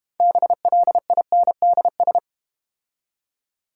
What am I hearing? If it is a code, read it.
Morse code, heard as a keyed tone, about 31 wpm, 700 Hz. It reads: BLINDS
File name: Easy Words - Speed-Racing with Hints - 25wpm